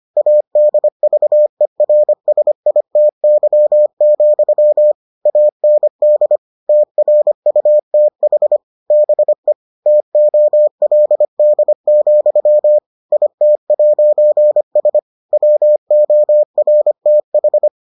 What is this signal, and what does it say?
Morse code, heard as a keyed tone, about 25 wpm, 600 Hz. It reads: ADVERSITY, AND TRUTH BE TOLD, IT'S WORTH